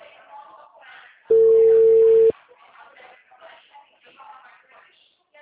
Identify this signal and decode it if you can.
Ringback tone (China, 450 Hz), with speech.